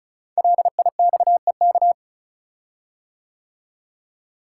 Morse code, heard as a keyed tone, about 36 wpm, 700 Hz. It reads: LIXEK
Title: Code Groups - 5 Letter:Numbers - 35wpm